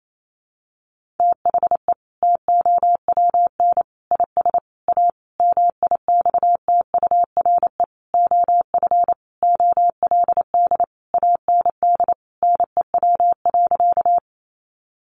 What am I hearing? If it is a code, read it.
Morse code, heard as a keyed tone, about 28 wpm, 700 Hz. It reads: THE TOWN IS A MIXTURE OF OLD AND NEW.
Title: Sentences from Top 1500 Words 28wpm